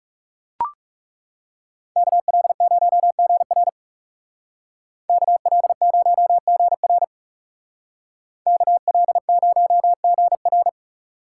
Morse code, heard as a keyed tone, about 37 wpm, 700 Hz. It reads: KL0GR KL0GR KL0GR
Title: Speed Racing - USA 2x2 Call Signs 30wpm